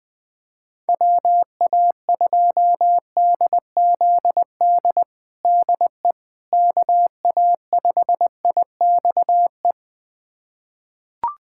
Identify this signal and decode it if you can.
Morse code, heard as a keyed tone, about 21 wpm, 700 Hz. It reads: WA2DZD DE KA5IXE
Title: USA Callsign DE Callsign - 6 Characters - 20wpm